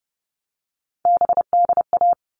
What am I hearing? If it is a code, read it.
Morse code, heard as a keyed tone, about 30 wpm, 700 Hz. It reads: BDA
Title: All Cut Numbers - 3-Digit 30wpm